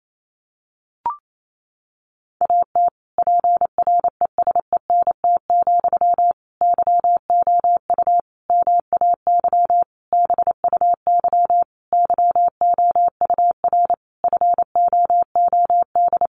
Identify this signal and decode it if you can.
Morse code, heard as a keyed tone, about 28 wpm, 700 Hz. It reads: AT PRESENT, YOU MAY BUY YOUR FOOD
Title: Sentences from Top 800 Words 28wpm